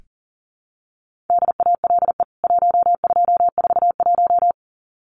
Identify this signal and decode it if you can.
Morse code, heard as a keyed tone, about 40 wpm, 700 Hz. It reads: DALE 1241
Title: CWT Exchange 40wpm